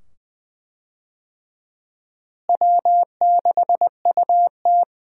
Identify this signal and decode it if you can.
Morse code, heard as a keyed tone, about 20 wpm, 700 Hz. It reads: W6UT
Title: USA Call Signs - 4 Characters - Repeated 2x - 20wpm